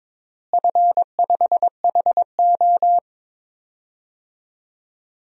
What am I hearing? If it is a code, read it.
Morse code, heard as a keyed tone, about 22 wpm, 700 Hz. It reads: F5HO
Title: DXCC-100 Call Signs - 4 Characters - Repeated 3x - 22wpm